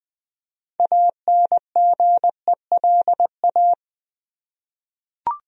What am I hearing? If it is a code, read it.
Morse code, heard as a keyed tone, about 20 wpm, 700 Hz. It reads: ANGELA
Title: USA Common Names - 20wpm